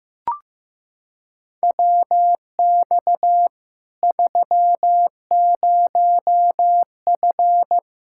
Morse code, heard as a keyed tone, about 15 wpm, 700 Hz. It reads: WX 30F
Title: Practice QSO - Exchange 2 - Info - 15wpm